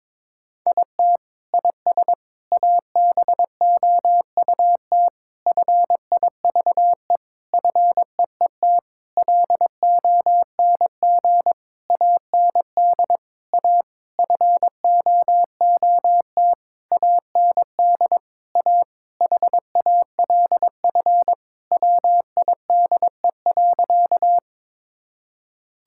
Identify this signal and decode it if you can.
Morse code, heard as a keyed tone, about 22 wpm, 700 Hz. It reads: IT IS ABOUT FIVE FEET LONG AND A FOOT AND A HALF WIDE.
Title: Sentences from Top 1000 Words - Review Part 6 - 22wpm